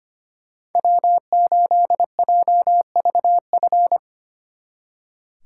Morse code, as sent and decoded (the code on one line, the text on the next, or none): .-- ---.. .--- ...- ..-.
W8JVF